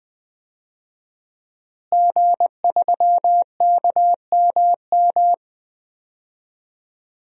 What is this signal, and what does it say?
Morse code, heard as a keyed tone, about 20 wpm, 700 Hz. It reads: G3KMM
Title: DXCC-100 Call Signs - 5 Characters - Repeated 3x - 20wpm